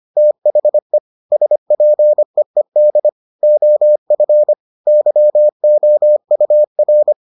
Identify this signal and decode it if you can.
Morse code, heard as a keyed tone, about 25 wpm, 600 Hz. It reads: THE SPEED OF YOUR